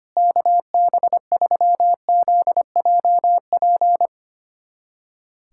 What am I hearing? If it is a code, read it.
Morse code, heard as a keyed tone, about 25 wpm, 700 Hz. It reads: KB3ZJP
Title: USA Callsign DE Callsign 25wpm